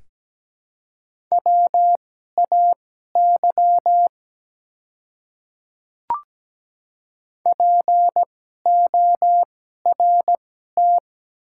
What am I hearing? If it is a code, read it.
Morse code, heard as a keyed tone, about 17 wpm, spaced as for 12 wpm, 700 Hz. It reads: WAY PORT
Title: 2x Character Spacing - Top 200 Words - 17wpm